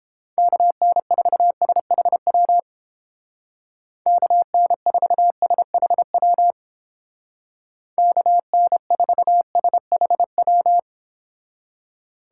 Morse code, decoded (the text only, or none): KN4SHW KN4SHW KN4SHW